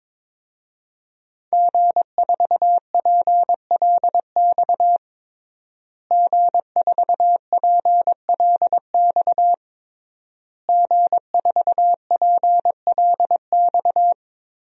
Morse code, heard as a keyed tone, about 22 wpm, 700 Hz. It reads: G4PLX G4PLX G4PLX
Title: DXCC-100 Call Signs - 5 Characters - Repeated 3x - 22wpm